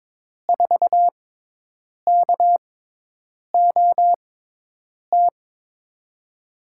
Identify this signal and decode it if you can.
Morse code, heard as a keyed tone, about 22 wpm, spaced as for 8 wpm, 700 Hz. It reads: 4KOT
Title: USA Call Signs - 6 Characters - 6x Character Spacing - 22wpm